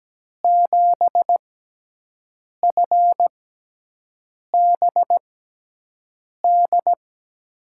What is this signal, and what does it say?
Morse code, heard as a keyed tone, about 17 wpm, spaced as for 6 wpm, 700 Hz. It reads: 7FBD
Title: USA Call Signs - 6x Character Spacing - 17wpm